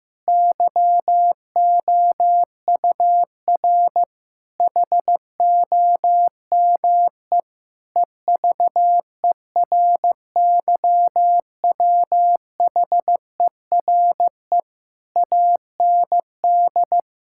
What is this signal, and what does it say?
Morse code, heard as a keyed tone, about 16 wpm, 700 Hz. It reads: YOUR HOME EVERYWHERE AND
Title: Sentences from Top 1600 Words 15wpm